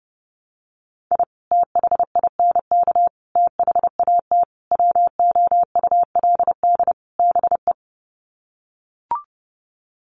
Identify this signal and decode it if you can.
Morse code, heard as a keyed tone, about 30 wpm, 700 Hz. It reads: I THINK THAT WOULD BE
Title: Sets of 5 Words Encore2 30wpm